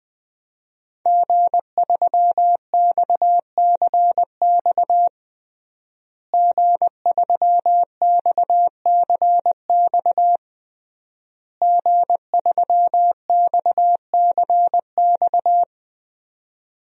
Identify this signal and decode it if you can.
Morse code, heard as a keyed tone, about 20 wpm, 700 Hz. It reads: G3XCX G3XCX G3XCX
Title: DXCC-100 Call Signs - 5 Characters - Repeated 3x - 20wpm